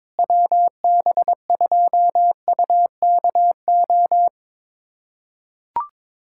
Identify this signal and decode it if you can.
Morse code, heard as a keyed tone, about 22 wpm, 700 Hz. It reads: WB2UKO
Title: USA Callsign DE Callsign - 6 Characters - 22wpm